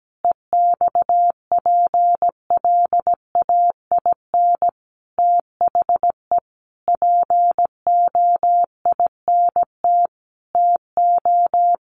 Morse code, heard as a keyed tone, about 17 wpm, 700 Hz. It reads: EXPLAIN THE POINT TO
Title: Sentences from Top 1100 Words 17wpm